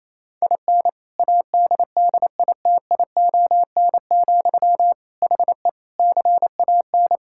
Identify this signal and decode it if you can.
Morse code, heard as a keyed tone, about 28 wpm, 700 Hz. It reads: IN ADDITION, HE CAN
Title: Sentences from Top 1000 Words - Review Part 6 - 28wpm